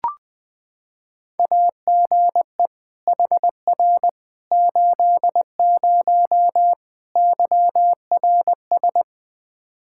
Morse code, heard as a keyed tone, about 20 wpm, 700 Hz. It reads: AGE HR 80 YRS
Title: K1USN SST QSO Practice 20wpm